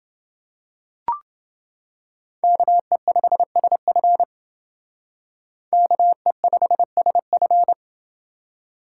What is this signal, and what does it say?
Morse code, heard as a keyed tone, about 28 wpm, 700 Hz. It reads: KE5SF KE5SF